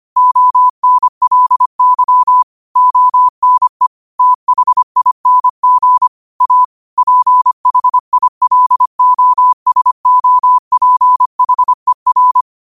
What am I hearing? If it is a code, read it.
Morse code, heard as a keyed tone, about 25 wpm, 1000 Hz. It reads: ONLY ONE THING A PHILOSOPHER
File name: qotdfiles / 25wpm